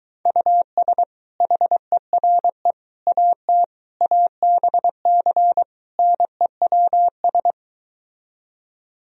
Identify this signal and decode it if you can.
Morse code, heard as a keyed tone, about 22 wpm, 700 Hz. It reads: US HERE AT ABC NEWS